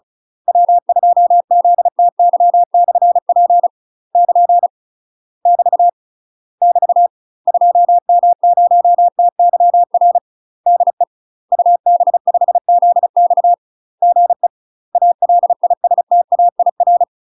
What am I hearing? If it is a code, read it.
Morse code, heard as a keyed tone, about 34 wpm, 700 Hz. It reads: W2ZTY/P <KN> <BT> <BT> 2M0TYR DE UB5ZX GE ALISTAIR
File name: UK MASTER QSOs TRAINING / 35WPM / UK MASTER QSOs 0024